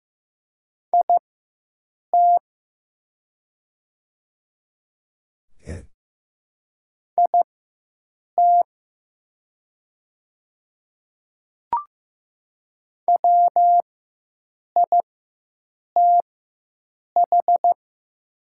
Morse code, heard as a keyed tone, about 15 wpm, spaced as for 7 wpm, 700 Hz. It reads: IT IT WITH
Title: 4x Character Spacing - Top 200 Words - 15wpm